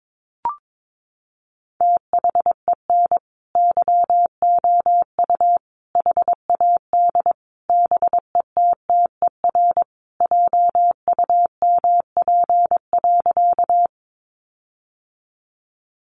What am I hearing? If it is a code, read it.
Morse code, heard as a keyed tone, about 22 wpm, 700 Hz. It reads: THEN YOU HAD BETTER JUMP.
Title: Sentences from Top 1300 Words 22wpm